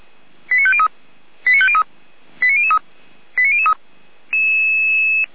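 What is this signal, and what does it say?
Five-tone selective call (ZVEI), heard as quick runs of five tones, tones of about 70 ms. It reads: 89503 89503 89003 89003
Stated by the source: Five-tone ZVEI selective call received on SDR.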